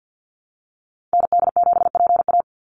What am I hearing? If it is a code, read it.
Morse code, heard as a keyed tone, about 48 wpm, 700 Hz. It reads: ND7PA
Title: NA SOTA Activator Callsigns 50wpm